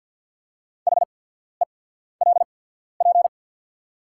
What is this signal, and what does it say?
Morse code, heard as a keyed tone, about 48 wpm, spaced as for 14 wpm, 690 Hz. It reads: HELP
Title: 8x Character Spacing - Top 300 Words - 50wpm